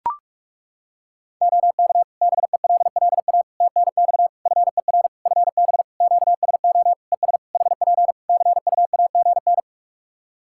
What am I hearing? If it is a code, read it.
Morse code, heard as a keyed tone, about 41 wpm, 700 Hz. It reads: OK BELLA TNX FER FB QSO ES HP CUAGN
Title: K1USN SST QSO Practice 45wpm